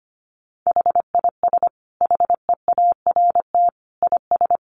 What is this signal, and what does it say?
Morse code, heard as a keyed tone, about 25 wpm, 700 Hz. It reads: HIS HEART IS
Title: Sentences from Top 1200 Words 25wpm